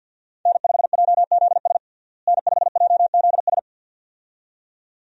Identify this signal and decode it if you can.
Morse code, heard as a keyed tone, about 52 wpm, 700 Hz. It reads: N5JZS N5JZS
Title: USA Call Signs - 5 Characters - Repeated 2x - 50wpm